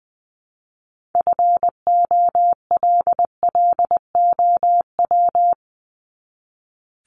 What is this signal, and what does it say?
Morse code, heard as a keyed tone, about 20 wpm, 700 Hz. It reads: FOLLOW
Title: ING Suffix - Top 100 Words with Base Words 20wpm